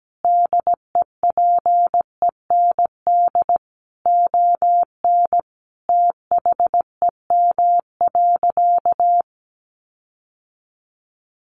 Morse code, heard as a keyed tone, about 17 wpm, 700 Hz. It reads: DEPEND ON THEM.
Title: Sentences from Top 1300 Words 17wpm